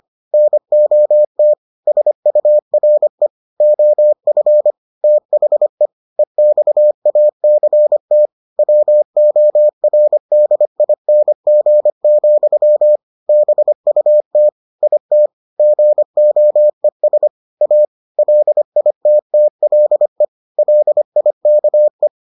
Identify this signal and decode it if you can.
Morse code, heard as a keyed tone, about 25 wpm, 600 Hz. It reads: NOT SURE OF THE EXACT WORDING, BUT IT GOES A LITTLE LIKE